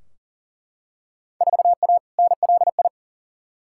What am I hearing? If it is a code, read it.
Morse code, heard as a keyed tone, about 40 wpm, 700 Hz. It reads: 4A NLI
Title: Field Day Exchanges Encore - 40wpm